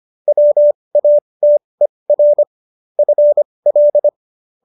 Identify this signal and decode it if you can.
Morse code, heard as a keyed tone, about 25 wpm, spaced as for 20 wpm, 590 Hz. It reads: WATER FL